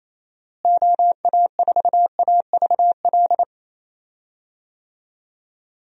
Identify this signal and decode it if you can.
Morse code, heard as a keyed tone, about 28 wpm, 700 Hz. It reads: OA4AVL
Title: USA Call Signs - 6 Characters - Repeated 3x - 28wpm